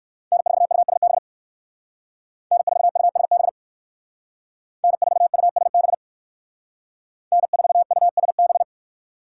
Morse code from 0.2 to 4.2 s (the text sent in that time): N4USB N4USB